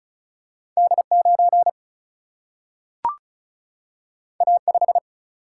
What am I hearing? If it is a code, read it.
Morse code, heard as a keyed tone, about 36 wpm, 700 Hz. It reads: D9 A5
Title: Single-Letter Single-Number 35wpm